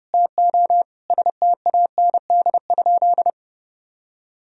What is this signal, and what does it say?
Morse code, heard as a keyed tone, about 30 wpm, 700 Hz. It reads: TO STAND?
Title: Sentences from Top 1000 Words - Review Part 3 - 30wpm